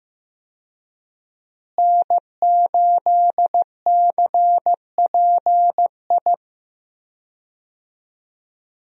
Morse code, heard as a keyed tone, about 15 wpm, 700 Hz. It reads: N8CPI
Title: USA Call Signs - 5 Characters - Repeated 2x - 15wpm